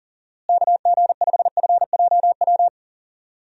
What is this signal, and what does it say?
Morse code, heard as a keyed tone, about 40 wpm, 700 Hz. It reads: KG5FJW
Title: USA Call Signs - 6 Characters - Repeated 2x - 40wpm